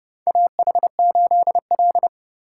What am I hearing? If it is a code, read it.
Morse code, heard as a keyed tone, about 30 wpm, 700 Hz. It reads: AH8L
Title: USA Call Signs - 4 Characters - 3x Character Spacing - 30wpm